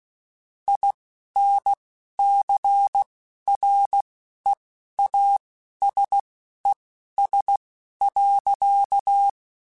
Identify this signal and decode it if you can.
Morse code, heard as a keyed tone, about 16 wpm, spaced as for 11 wpm, 800 Hz. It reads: INCREASES.
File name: Technical Subjects / Technical Subjects-mp3s / XL @10